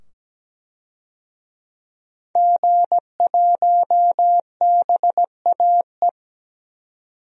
Morse code, heard as a keyed tone, about 17 wpm, 700 Hz. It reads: G1BAE
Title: DXCC-100 Call Signs - 5 Characters - 17wpm